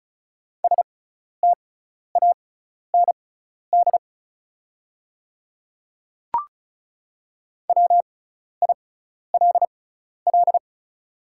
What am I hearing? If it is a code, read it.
Morse code, heard as a keyed tone, about 36 wpm, spaced as for 12 wpm, 690 Hz. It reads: STAND WILL
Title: 6x Character Spacing - Top 300 Words - 35wpm